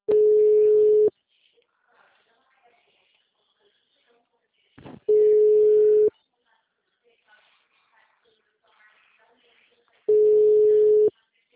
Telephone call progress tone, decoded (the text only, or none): ringback tone